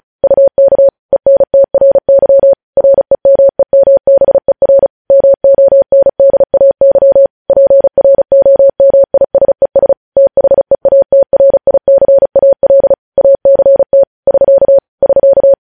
Morse code, heard as a keyed tone, about 36 wpm, 550 Hz. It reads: UK ENTRY REMEMBER MONDAY PROMISES THEATRICAL ACT <SK> <SK>
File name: podcast / News headlines-1x-35x35